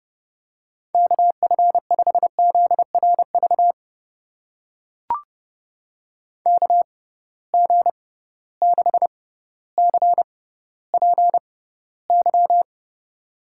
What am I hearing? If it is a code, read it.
Morse code, heard as a keyed tone, about 30 wpm, spaced as for 15 wpm, 700 Hz. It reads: KF5ZRV KG6CPY
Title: USA Call Signs - 6 Characters - 6x Character Spacing - 30wpm